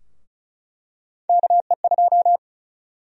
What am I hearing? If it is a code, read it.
Morse code, heard as a keyed tone, about 36 wpm, 700 Hz. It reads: KE2
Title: USA Call Sign Prefixes - 35wpm